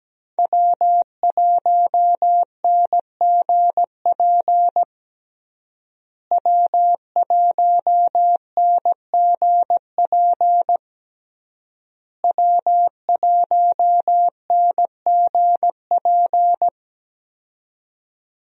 Morse code, heard as a keyed tone, about 17 wpm, 700 Hz. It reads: W1NGP W1NGP W1NGP
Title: USA Call Signs - 5 Characters - Repeated 3x - 17wpm